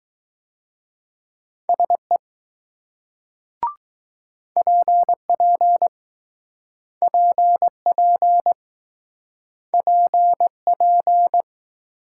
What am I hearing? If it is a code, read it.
Morse code, heard as a keyed tone, about 20 wpm, 700 Hz. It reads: SE PP PP PP